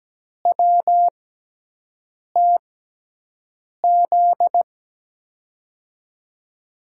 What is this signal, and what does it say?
Morse code, heard as a keyed tone, about 17 wpm, spaced as for 6 wpm, 700 Hz. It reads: WTZ
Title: USA Call Signs - 6 Characters - 6x Character Spacing - 17wpm